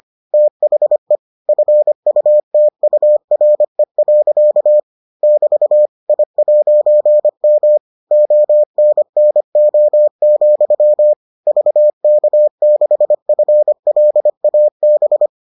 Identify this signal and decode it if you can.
Morse code, heard as a keyed tone, about 25 wpm, 600 Hz. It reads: THE FUTURE. <BT> I'M ONNO, VK6FLAB